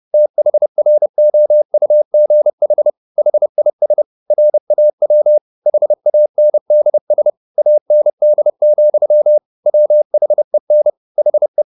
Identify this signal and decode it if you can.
Morse code, heard as a keyed tone, about 30 wpm, 600 Hz. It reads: THROUGH HIS RAW HANDS AND, WHEN HE